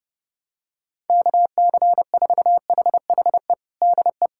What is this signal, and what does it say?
Morse code, heard as a keyed tone, about 30 wpm, 700 Hz. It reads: KC4HHE DE
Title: USA Callsign DE Callsign - 6 Characters - 30wpm